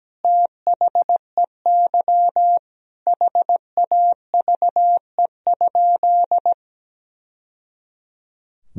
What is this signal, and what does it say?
Morse code, heard as a keyed tone, about 17 wpm, 700 Hz. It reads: THEY HAVE?
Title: Sentences from Top 1000 Words - Review Part 3 - 17wpm